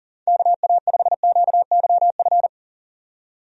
Morse code, heard as a keyed tone, about 40 wpm, 700 Hz. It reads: KA5QYF
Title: USA Call Signs - 6 Characters - 40wpm